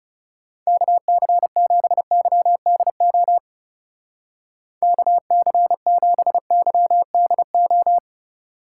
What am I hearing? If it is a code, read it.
Morse code, heard as a keyed tone, about 38 wpm, 700 Hz. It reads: KC7YDO KC7YDO